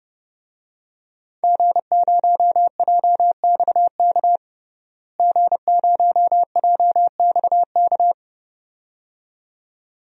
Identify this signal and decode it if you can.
Morse code, heard as a keyed tone, about 32 wpm, 700 Hz. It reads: G0JXK G0JXK
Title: DXCC-100 Call Signs - 5 Characters - Repeated 2x - 30wpm